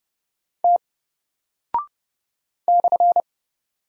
Morse code, heard as a keyed tone, about 30 wpm, 700 Hz. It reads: T /
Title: Single Character (Letter:Number:Punctuation:Prosign) Alt 30wpm